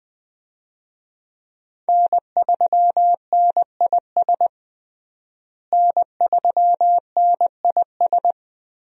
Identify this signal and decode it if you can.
Morse code, heard as a keyed tone, about 20 wpm, 700 Hz. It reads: N3NIS N3NIS
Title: USA Call Signs - 5 Characters - Repeated 3x - 20wpm